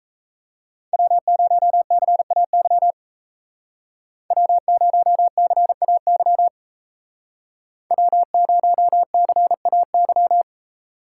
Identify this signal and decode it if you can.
Morse code, heard as a keyed tone, about 37 wpm, 700 Hz. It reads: W0CAY W0CAY W0CAY